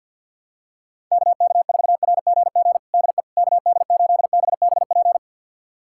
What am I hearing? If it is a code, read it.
Morse code, heard as a keyed tone, about 46 wpm, 700 Hz. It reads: KK4RGG DE KD8BBP
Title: USA Callsign DE Callsign 50wpm